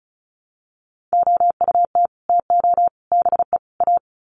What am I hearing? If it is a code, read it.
Morse code, heard as a keyed tone, about 34 wpm, 700 Hz. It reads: OUT TO BE A
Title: Sets of 4 Words 35wpm